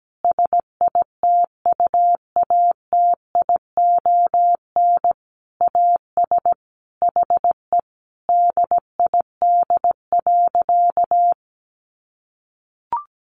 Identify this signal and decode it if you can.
Morse code, heard as a keyed tone, about 17 wpm, 700 Hz. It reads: SITUATION AS HE DID.
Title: Sentences from Top 1200 Words 17wpm